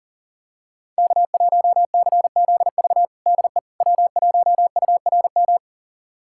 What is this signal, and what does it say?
Morse code, heard as a keyed tone, about 37 wpm, 700 Hz. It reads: K1CZV DE W1URM
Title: USA Callsign DE Callsign - 5 Characters - 40wpm